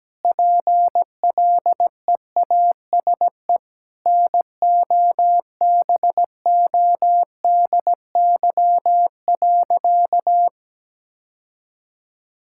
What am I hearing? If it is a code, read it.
Morse code, heard as a keyed tone, about 17 wpm, 700 Hz. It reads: PLEASE NOBODY.
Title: Sentences from Top 1600 Words 17wpm